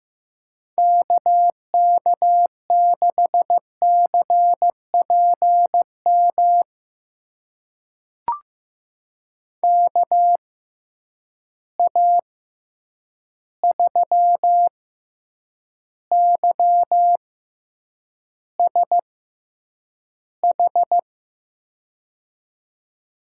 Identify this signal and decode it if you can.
Morse code, heard as a keyed tone, about 15 wpm, spaced as for 8 wpm, 700 Hz. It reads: KK6CPM KA3YSH
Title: USA Call Signs - 6 Characters - 6x Character Spacing - 15wpm